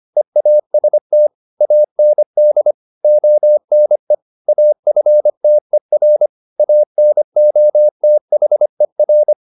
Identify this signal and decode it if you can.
Morse code, heard as a keyed tone, about 25 wpm, 600 Hz. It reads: EAST AND ONE AFTER ANOTHER